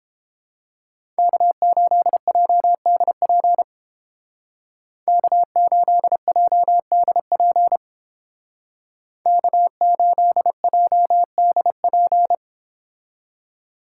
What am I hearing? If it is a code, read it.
Morse code, heard as a keyed tone, about 30 wpm, 700 Hz. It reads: K8JDP K8JDP K8JDP